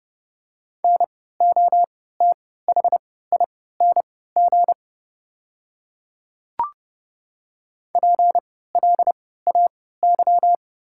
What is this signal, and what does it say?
Morse code, heard as a keyed tone, about 29 wpm, spaced as for 17 wpm, 700 Hz. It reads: NOTHING PLAY
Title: 3x Character Spacing - Top 500 Words - 30wpm